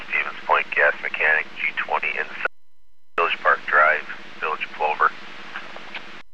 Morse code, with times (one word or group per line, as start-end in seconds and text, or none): none